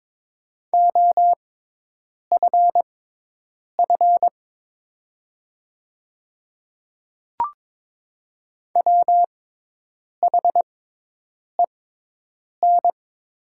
Morse code, heard as a keyed tone, about 22 wpm, spaced as for 8 wpm, 700 Hz. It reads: OFF WHEN
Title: 6x Character Spacing - Top 300 Words - 22wpm